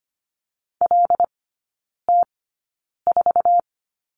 Morse code, heard as a keyed tone, about 25 wpm, spaced as for 9 wpm, 700 Hz. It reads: LT4
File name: Lesson 17e - Letter u - Three Characters 25wpm